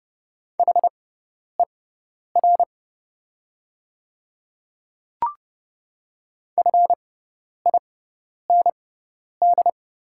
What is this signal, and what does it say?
Morse code, heard as a keyed tone, about 30 wpm, spaced as for 10 wpm, 700 Hz. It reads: HER FIND
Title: 6x Character Spacing - Top 100 Words - 30wpm